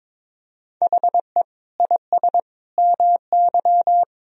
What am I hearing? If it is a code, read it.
Morse code, heard as a keyed tone, about 22 wpm, 700 Hz. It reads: HE IS MY
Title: Sentences from Top 1000 Words - Review Part 2 - 22wpm